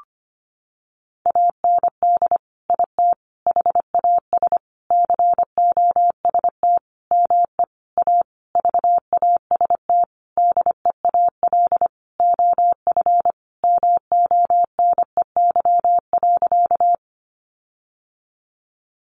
Morse code, heard as a keyed tone, about 25 wpm, 700 Hz. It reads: AND IT HAS COST ME A VAST DEAL OF MONEY.
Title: Sentences from Top 1300 Words 25wpm